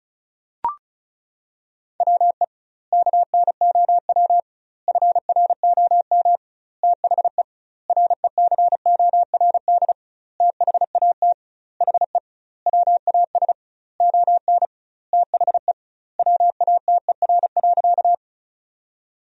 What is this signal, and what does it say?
Morse code, heard as a keyed tone, about 34 wpm, 700 Hz. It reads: WE KNOW FROM THE RECORD THAT HE WAS ON THE WATER.
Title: 2x Word-Spacing - Sentences from Top 500 Words - 35wpm